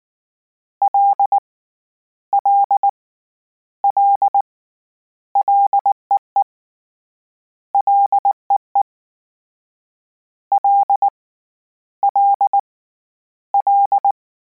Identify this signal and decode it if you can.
Morse code, heard as a keyed tone, about 19 wpm, 800 Hz. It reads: L L L LEE LEE L L L
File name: Combinations / Combinations-mp3s / Combinations@18